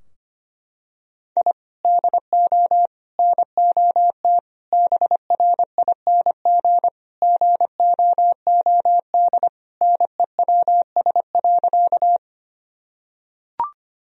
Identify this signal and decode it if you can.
Morse code, heard as a keyed tone, about 25 wpm, 700 Hz. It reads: I DO NOT BRING GOOD NEWS.
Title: Sentences from Top 1000 Words - Review Part 3 - 25wpm